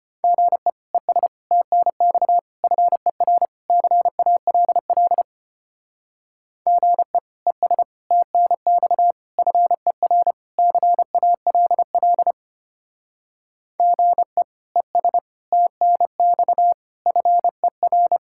Morse code, heard as a keyed tone, about 31 wpm, 700 Hz. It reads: GE ES TNX FER CALL GE ES TNX FER CALL GE ES TNX FER
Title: Speed-Racing Common QSO Phrases - 25wpm